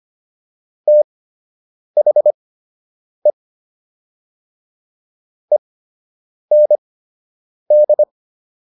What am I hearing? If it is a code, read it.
Morse code, heard as a keyed tone, about 25 wpm, spaced as for 8 wpm, 600 Hz. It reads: THE END